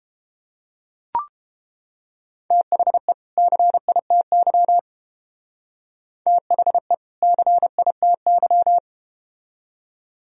Morse code, heard as a keyed tone, about 30 wpm, 700 Hz. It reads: THE CITY THE CITY